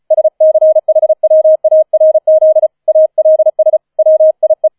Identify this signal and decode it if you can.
Morse code, heard as a keyed tone, about 34 wpm, 630 Hz. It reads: SCHWARZ ALS WIE
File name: 004 Vierter Streich 35wpm